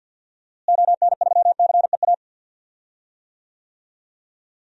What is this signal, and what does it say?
Morse code, heard as a keyed tone, about 53 wpm, 700 Hz. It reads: ON3CEA
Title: DXCC-100 Call Signs - 6 Characters - Repeated 3x - 50wpm